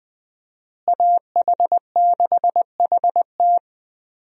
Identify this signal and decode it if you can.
Morse code, heard as a keyed tone, about 20 wpm, 700 Hz. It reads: AH6HT
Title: USA Call Signs - 5 Characters - Repeated 3x - 20wpm